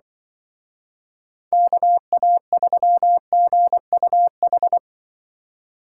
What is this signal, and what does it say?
Morse code, heard as a keyed tone, about 25 wpm, 700 Hz. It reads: KA3GUH
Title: Speed Racing - USA 2x3 Call Signs 20wpm